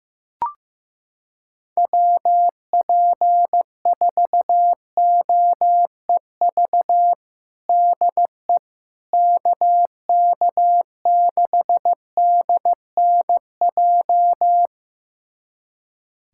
Morse code, heard as a keyed tone, about 15 wpm, 700 Hz. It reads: WP4OEV DE KK6DNJ
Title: USA Callsign DE Callsign - 6 Characters - 15wpm